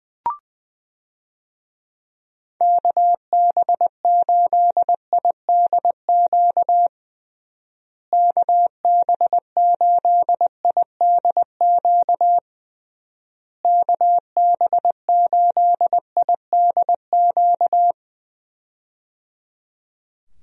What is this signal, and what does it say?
Morse code, heard as a keyed tone, about 20 wpm, 700 Hz. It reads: KB8IDQ KB8IDQ KB8IDQ
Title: USA Call Signs - 6 Characters - Repeated 3x - 20wpm